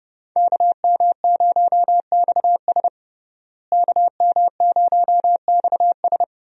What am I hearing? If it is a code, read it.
Morse code, heard as a keyed tone, about 32 wpm, 700 Hz. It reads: KM0XS KM0XS
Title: USA Call Signs - 5 Characters - Repeated 2x - 30wpm